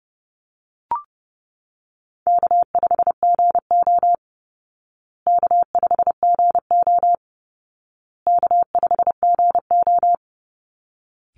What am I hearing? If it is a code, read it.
Morse code, heard as a keyed tone, about 30 wpm, 700 Hz. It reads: K5GO K5GO K5GO
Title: Call Signs Repeated 2x - 30wpm